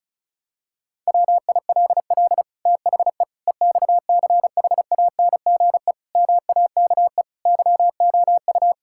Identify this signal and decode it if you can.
Morse code, heard as a keyed tone, about 36 wpm, 700 Hz. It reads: WILL THE EXCHANGE MAKE YOU
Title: Sentences from Top 1000 Words - Review Part 6 - 35wpm